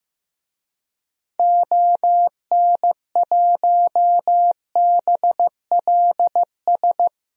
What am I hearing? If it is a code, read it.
Morse code, heard as a keyed tone, about 15 wpm, 700 Hz. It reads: ON1BLS
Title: DXCC-100 Call Signs - 6 Characters - 15wpm